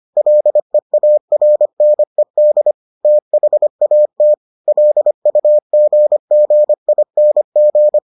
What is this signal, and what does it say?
Morse code, heard as a keyed tone, about 25 wpm, 600 Hz. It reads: LEARNED THAT LUGGING